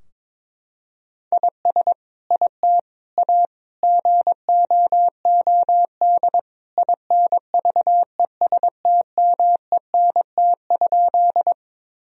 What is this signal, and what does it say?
Morse code, heard as a keyed tone, about 22 wpm, 700 Hz. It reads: IS IT A GOOD INVESTMENT?
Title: Sentences from Top 1000 Words - Review Part 4 - 22wpm